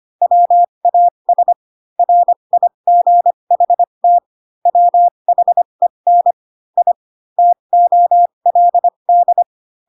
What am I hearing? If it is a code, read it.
Morse code, heard as a keyed tone, about 25 wpm, spaced as for 22 wpm, 700 Hz. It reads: WAS RIGHT WHEN I TOLD
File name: Pinocchio / Pinocchio ch7 18 25 ik1hge